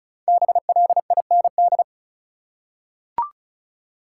Morse code, heard as a keyed tone, about 36 wpm, 700 Hz. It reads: BLIND
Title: Five Letter Words - 35wpm